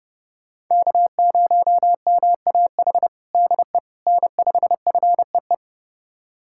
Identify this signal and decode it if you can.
Morse code, heard as a keyed tone, about 30 wpm, 700 Hz. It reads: K0MAH DE N5FEE
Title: USA Callsign DE Callsign 30wpm